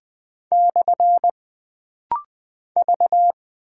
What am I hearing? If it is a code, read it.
Morse code, heard as a keyed tone, about 20 wpm, 700 Hz. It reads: / V
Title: Single Character (Letter:Number:Punctuation:Prosign) - Rapid-Fire - 20wpm